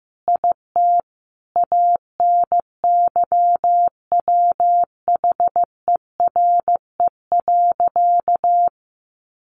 Morse code, heard as a keyed tone, about 15 wpm, 700 Hz. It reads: IT ANYWHERE.
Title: Sentences from Top 1400 Words 15wpm